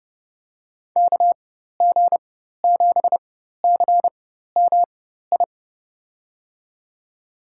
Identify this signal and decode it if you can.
Morse code, heard as a keyed tone, about 30 wpm, spaced as for 14 wpm, 700 Hz. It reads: KG7CMI
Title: USA Call Signs - 6 Characters - 4x Character Spacing - 30wpm